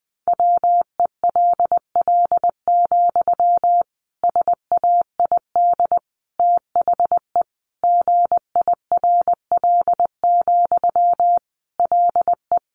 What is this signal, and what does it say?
Morse code, heard as a keyed tone, about 20 wpm, 700 Hz. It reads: WELL, SAID THE GIRL, LE
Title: Sentences from Top 400 Words 20wpm